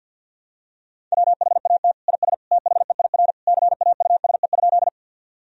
Morse code, heard as a keyed tone, about 53 wpm, 700 Hz. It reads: WHAT IS THEIR CAUSE?
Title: Sentences from Top 1000 Words - Review Part 1 - 50wpm